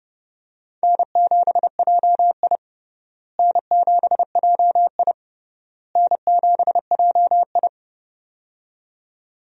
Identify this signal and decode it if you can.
Morse code, heard as a keyed tone, about 30 wpm, 700 Hz. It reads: N7JI N7JI N7JI
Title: USA Call Signs - 4 Characters - Repeated 3x - 30wpm